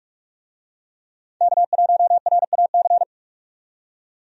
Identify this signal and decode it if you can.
Morse code, heard as a keyed tone, about 46 wpm, 700 Hz. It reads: K1RAC
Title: USA Call Signs - 5 Characters - 45wpm